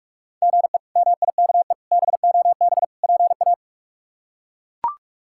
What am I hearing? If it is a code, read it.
Morse code, heard as a keyed tone, about 41 wpm, 700 Hz. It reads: GE MIKE BOB PA
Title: K1USN SST QSO Practice - 45wpm